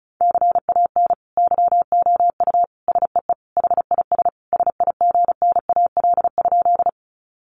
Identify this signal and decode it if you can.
Morse code, heard as a keyed tone, about 34 wpm, 700 Hz. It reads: CAN YOU SEE HIS SIGNAL?
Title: Sentences from Top 1400 Words 35wpm